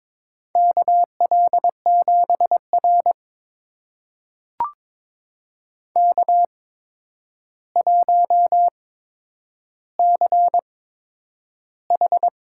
Morse code, heard as a keyed tone, about 22 wpm, spaced as for 9 wpm, 700 Hz. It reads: KL7R K1CH
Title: USA Call Signs - 4 Characters - 8x Character Spacing - 22wpm